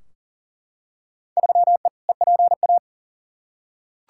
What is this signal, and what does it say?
Morse code, heard as a keyed tone, about 40 wpm, 700 Hz. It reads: 3E EPA
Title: Field Day Exchanges Encore - 40wpm